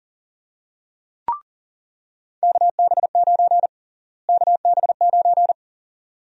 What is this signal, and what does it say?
Morse code, heard as a keyed tone, about 40 wpm, 700 Hz. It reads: KB9 KB9
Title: USA Call Sign Prefixes - Repeated 2x - 40wpm